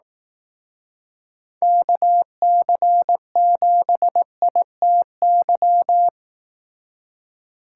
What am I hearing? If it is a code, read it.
Morse code, heard as a keyed tone, about 18 wpm, 700 Hz. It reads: KC7ITY
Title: Speed Racing - USA 2x3 Call Signs 15wpm